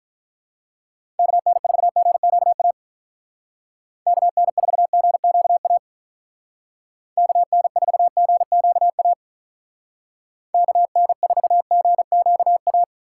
Morse code, heard as a keyed tone, about 43 wpm, 700 Hz. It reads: KN4GQA KN4GQA KN4GQA KN4GQA
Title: Speed Racing - USA 2x3 Call Signs 35wpm